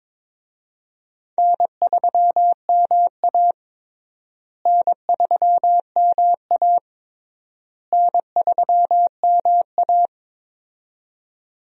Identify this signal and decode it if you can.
Morse code, heard as a keyed tone, about 22 wpm, 700 Hz. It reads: N3MA N3MA N3MA
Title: USA Call Signs - 4 Characters - Repeated 3x - 22wpm